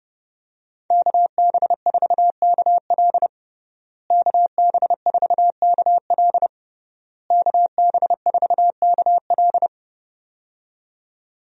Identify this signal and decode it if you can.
Morse code, heard as a keyed tone, about 30 wpm, 700 Hz. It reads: KB4KL KB4KL KB4KL
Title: USA Call Signs - 5 Characters - Repeated 3x - 30wpm